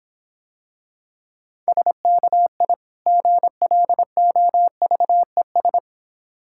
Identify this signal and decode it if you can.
Morse code, heard as a keyed tone, about 26 wpm, 700 Hz. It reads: SKI GLOVES